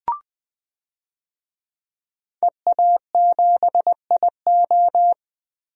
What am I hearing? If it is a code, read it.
Morse code, heard as a keyed tone, about 21 wpm, 700 Hz. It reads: EA7IO
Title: DXCC-100 Call Signs - 5 Characters - Repeated 3x - 20wpm